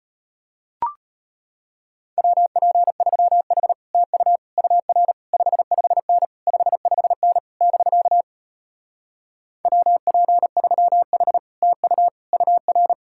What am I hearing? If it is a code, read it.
Morse code, heard as a keyed tone, about 33 wpm, 700 Hz. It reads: WP3H TU UR 55N 55N <BK> WP3H TU UR